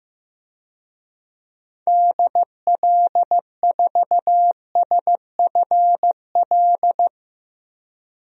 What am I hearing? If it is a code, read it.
Morse code, heard as a keyed tone, about 15 wpm, 700 Hz. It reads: DL4SFL
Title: DXCC-100 Call Signs - 6 Characters - 15wpm